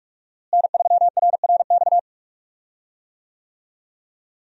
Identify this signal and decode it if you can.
Morse code, heard as a keyed tone, about 46 wpm, 700 Hz. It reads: N3RRX
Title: USA Call Signs - 5 Characters - Repeated 2x - 45wpm